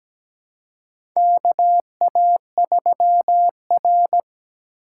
Morse code, heard as a keyed tone, about 17 wpm, 700 Hz. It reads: KA3R
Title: USA Call Signs - 2x Character Spacing - 17wpm